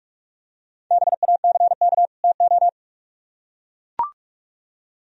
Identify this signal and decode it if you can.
Morse code, heard as a keyed tone, about 44 wpm, 700 Hz. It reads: BACK TO